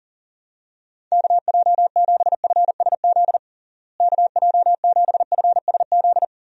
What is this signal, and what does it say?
Morse code, heard as a keyed tone, about 40 wpm, 700 Hz. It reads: KJ7FSZ KJ7FSZ
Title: USA Call Signs - 6 Characters - Repeated 2x - 40wpm